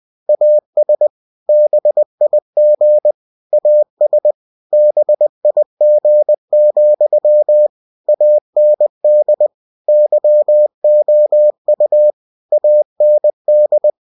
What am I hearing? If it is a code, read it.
Morse code, heard as a keyed tone, about 20 wpm, 600 Hz. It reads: AS BIG AS BIG, AND YOU AND
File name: cw-books / winnie-the-pooh / winnie-20wpm-page-0006